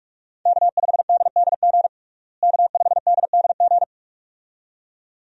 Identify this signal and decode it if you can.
Morse code, heard as a keyed tone, about 42 wpm, 700 Hz. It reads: K5DDG K5DDG
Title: USA Call Signs - 5 Characters - Repeated 2x - 45wpm